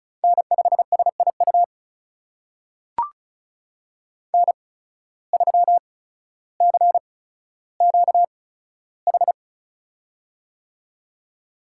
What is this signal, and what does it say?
Morse code, heard as a keyed tone, about 36 wpm, spaced as for 15 wpm, 700 Hz. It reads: N5SIU N3CQH
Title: USA Call Signs - 5 Characters - 8x Character Spacing - 35wpm